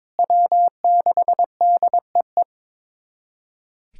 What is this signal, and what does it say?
Morse code, heard as a keyed tone, about 21 wpm, 700 Hz. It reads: W6DEE
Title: USA Callsign DE Callsign - 5 Characters - 22wpm